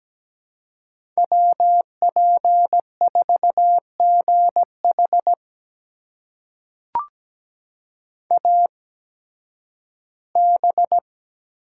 Morse code, heard as a keyed tone, about 17 wpm, spaced as for 11 wpm, 700 Hz. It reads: WP4GH AB
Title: USA Call Signs - 5 Characters - 8x Character Spacing - 17wpm